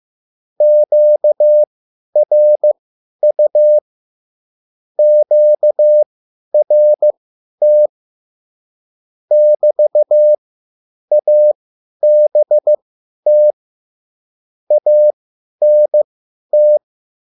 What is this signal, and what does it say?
Morse code, heard as a keyed tone, about 15 wpm, spaced as for 10 wpm, 600 Hz. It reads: QRU QRT <BT>ABT ANT